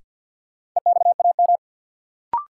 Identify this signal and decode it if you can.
Morse code, heard as a keyed tone, about 46 wpm, 700 Hz. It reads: EXAM
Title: Sentences from Top 1000 Words - Review Part 5 - New Word Drill - 50wpm